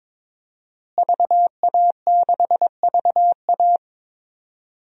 Morse code, heard as a keyed tone, about 22 wpm, 700 Hz. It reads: VA6VA
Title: DXCC-100 Call Signs - 5 Characters - Repeated 3x - 22wpm